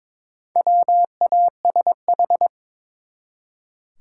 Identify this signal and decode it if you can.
Morse code, heard as a keyed tone, about 22 wpm, 700 Hz. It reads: WASH
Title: Four Letter Words - 22wpm